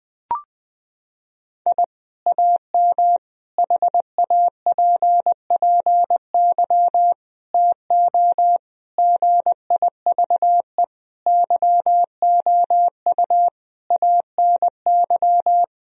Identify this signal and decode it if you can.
Morse code, heard as a keyed tone, about 20 wpm, 700 Hz. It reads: I AM HAPPY TO GIVE YOU ANY
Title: Sentences from Top 1000 Words - Review Part 3 - 20wpm